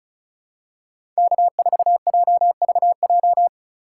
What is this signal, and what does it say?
Morse code, heard as a keyed tone, about 34 wpm, 700 Hz. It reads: K4JVJ
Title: USA Call Signs - 5 Characters - 4x Character Spacing - 35wpm